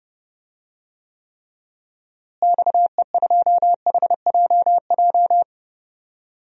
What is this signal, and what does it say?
Morse code, heard as a keyed tone, about 32 wpm, 700 Hz. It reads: XE2HJJ
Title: DXCC-100 Call Signs - 6 Characters - Repeated 3x - 30wpm